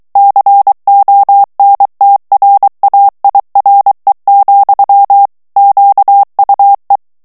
Morse code, heard as a keyed tone, about 23 wpm, 800 Hz. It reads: CONTRAIRE, QUE
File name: CW / FICHIERS-MP3 / FRANCAIS TEXTE CLAIR 2 VITESSE 22 MOTS MINUTES